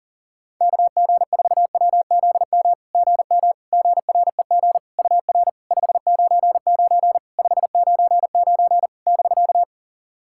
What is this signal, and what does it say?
Morse code, heard as a keyed tone, about 44 wpm, 700 Hz. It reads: KG4WZM GM GREG UR 599 599 <BK>
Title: SOTA QSO Practice 40wpm